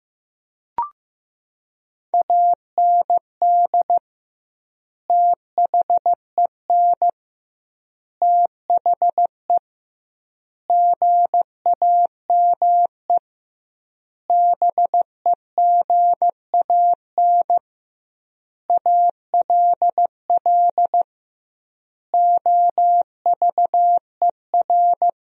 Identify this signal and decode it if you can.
Morse code, heard as a keyed tone, about 15 wpm, 700 Hz. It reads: AND THEN THE GAME BEGAN ALL OVER
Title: 2x Word-Spacing - Sentences from Top 500 Words - 15wpm